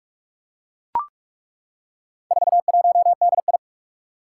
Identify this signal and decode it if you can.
Morse code, heard as a keyed tone, about 41 wpm, 700 Hz. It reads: 41DI
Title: Four Challenging Letter-Number Combinations 45wpm